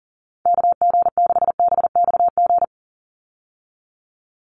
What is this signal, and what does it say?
Morse code, heard as a keyed tone, about 40 wpm, 700 Hz. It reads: KG6BXG
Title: USA Call Signs 40wpm